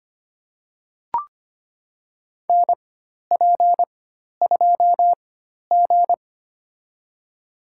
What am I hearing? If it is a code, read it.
Morse code, heard as a keyed tone, about 25 wpm, spaced as for 12 wpm, 700 Hz. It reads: NP2G
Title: USA Call Signs - 4 Characters - 4x Character Spacing - 25wpm